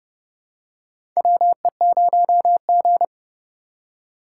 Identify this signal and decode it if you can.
Morse code, heard as a keyed tone, about 30 wpm, 700 Hz. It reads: WE0G
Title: USA Call Signs - 4x Character Spacing - 30wpm